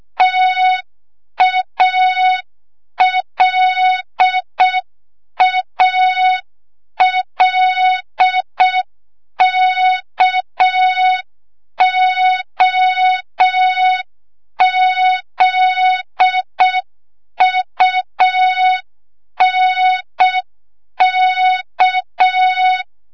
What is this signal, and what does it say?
Morse code, heard as a keyed tone, about 6 wpm, 3700 Hz. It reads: TALALKOZUNK
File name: Morze-HA2MN / hang / 25 GYAKORLAT